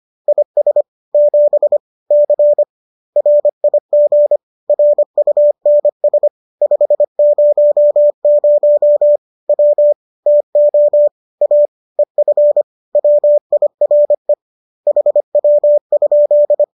Morse code, heard as a keyed tone, about 25 wpm, 590 Hz. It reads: IS 7 C RIG RUNS 500 W TO A EF WIRE HW?
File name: qso203 25